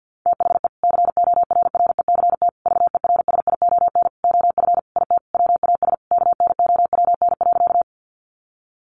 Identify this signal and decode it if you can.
Morse code, heard as a keyed tone, about 50 wpm, 700 Hz. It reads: THE CORRECT VERSION OF IT WAS KNOWN.
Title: Sentences from Top 1400 Words 50wpm